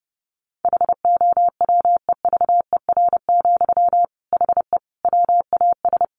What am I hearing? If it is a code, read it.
Morse code, heard as a keyed tone, about 30 wpm, 700 Hz. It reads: HOWEVER, HE WAS
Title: Sentences from Top 1500 Words 30wpm